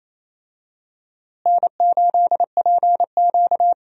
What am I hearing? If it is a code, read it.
Morse code, heard as a keyed tone, about 27 wpm, 700 Hz. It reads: N8PQ
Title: USA Call Signs - 4 Characters - Repeated 2x - 28wpm